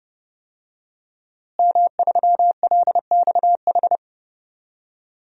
Morse code, heard as a keyed tone, about 30 wpm, 700 Hz. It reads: M3LXH
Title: DXCC-100 Call Signs - 5 Characters - Repeated 2x - 30wpm